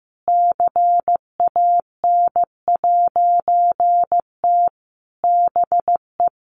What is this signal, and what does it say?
Morse code, heard as a keyed tone, about 14 wpm, 700 Hz. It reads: CAN'T BE
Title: A Sound Of Thunder 15wpm